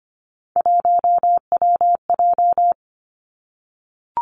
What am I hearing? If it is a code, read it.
Morse code, heard as a keyed tone, about 25 wpm, 700 Hz. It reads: 1WJ
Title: Three Challenging Letter-Number Combinations 25wpm